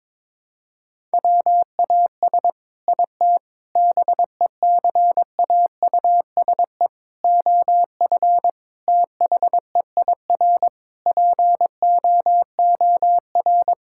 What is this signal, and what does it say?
Morse code, heard as a keyed tone, about 22 wpm, 700 Hz. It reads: WAS IT BECAUSE OF THEIR POOR
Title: Sentences from Top 1000 Words - Review Part 6 - 22wpm